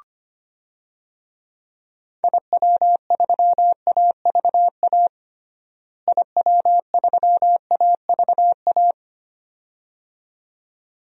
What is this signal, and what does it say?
Morse code, heard as a keyed tone, about 25 wpm, 700 Hz. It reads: IW3AVA IW3AVA
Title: DXCC-100 Call Signs - 6 Characters - Repeated 2x - 25wpm